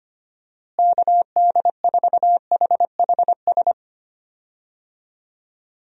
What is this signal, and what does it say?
Morse code, heard as a keyed tone, about 25 wpm, 700 Hz. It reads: KD4HHS
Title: USA Call Signs - 6 Characters - Repeated 2x - 25wpm